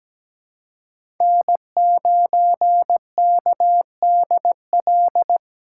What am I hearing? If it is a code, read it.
Morse code, heard as a keyed tone, about 17 wpm, 700 Hz. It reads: N9KDL
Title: USA Call Signs - 2x Character Spacing - 17wpm